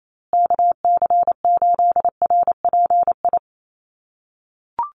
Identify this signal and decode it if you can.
Morse code, heard as a keyed tone, about 28 wpm, 700 Hz. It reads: KC8RPI
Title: USA Call Signs Repeated 2x 28wpm